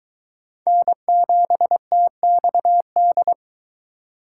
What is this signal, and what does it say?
Morse code, heard as a keyed tone, about 22 wpm, 700 Hz. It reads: N7TXD